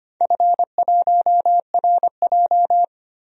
Morse code, heard as a keyed tone, about 25 wpm, 700 Hz. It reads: F1RJ
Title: DXCC-100 Call Signs - 4 Characters - 25wpm